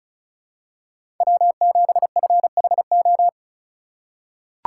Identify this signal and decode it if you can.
Morse code, heard as a keyed tone, about 36 wpm, 700 Hz. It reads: W7FHO
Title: USA Call Signs - 4x Character Spacing - 35wpm